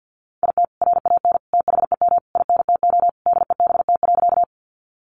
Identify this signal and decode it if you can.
Morse code, heard as a keyed tone, about 48 wpm, 700 Hz. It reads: IT RAN THEM INTO DEBT.
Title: Sentences from Top 1300 Words 50wpm